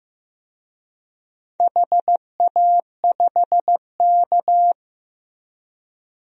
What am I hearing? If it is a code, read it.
Morse code, heard as a keyed tone, about 15 wpm, 700 Hz. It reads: HA5K
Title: DXCC-100 Call Signs - 4 Characters - Repeated 2x - 15wpm